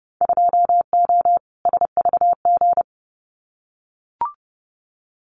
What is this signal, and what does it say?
Morse code, heard as a keyed tone, about 31 wpm, 700 Hz. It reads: 2O SVG
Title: Winter Field Day Exchanges 30wpm